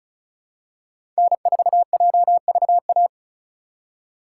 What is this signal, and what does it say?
Morse code, heard as a keyed tone, about 36 wpm, 700 Hz. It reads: N4JVA
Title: USA Call Signs - 5 Characters - 4x Character Spacing - 35wpm